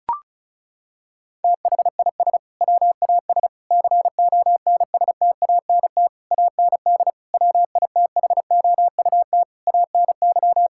Morse code, heard as a keyed tone, about 34 wpm, 700 Hz. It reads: THIS WAS CONSTANT AND WITHOUT ANY
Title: Sentences from Top 1600 Words 35wpm